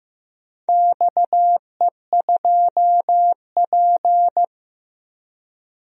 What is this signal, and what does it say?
Morse code, heard as a keyed tone, about 15 wpm, 700 Hz. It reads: XE2P
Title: DXCC-100 Call Signs - 4 Characters - 15wpm